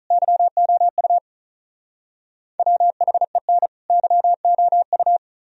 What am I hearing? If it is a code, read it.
Morse code, heard as a keyed tone, about 44 wpm, 700 Hz. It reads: YOU WHEN YOU
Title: Speed-Racing - Sets of 2 Words 35wpm